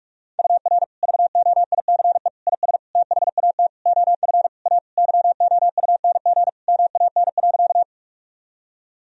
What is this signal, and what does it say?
Morse code, heard as a keyed tone, about 41 wpm, 700 Hz. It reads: UR VOICE IS THAT OF A YOUNG MAN.
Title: Sentences from Top 500 Words and Abbreviations 45wpm